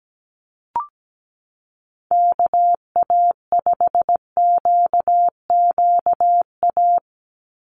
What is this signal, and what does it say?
Morse code, heard as a keyed tone, about 17 wpm, 700 Hz. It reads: KA5QQA
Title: NA SOTA Chaser Callsigns 17wpm